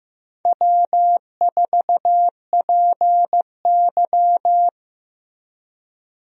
Morse code, heard as a keyed tone, about 15 wpm, 700 Hz. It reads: W4PY
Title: USA Call Signs - 4 Characters - Repeated 3x - 15wpm